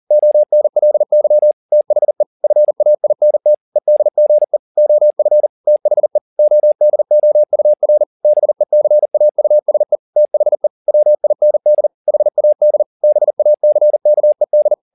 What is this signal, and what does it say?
Morse code, heard as a keyed tone, about 40 wpm, 600 Hz. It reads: ONLY THE FAINT EDGE OF THE ODOUR BECAUSE THE WIND HAD BACKED